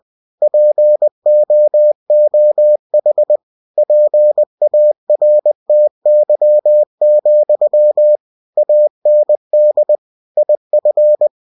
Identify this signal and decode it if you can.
Morse code, heard as a keyed tone, about 20 wpm, 600 Hz. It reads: POOH PARTY, AND IF
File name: cw-books / winnie-the-pooh / winnie-20wpm-page-0103